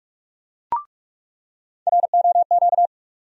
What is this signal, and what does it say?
Morse code, heard as a keyed tone, about 46 wpm, 700 Hz. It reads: ROQ
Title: USA Call Sign Postfixes - 3 Letters 45wpm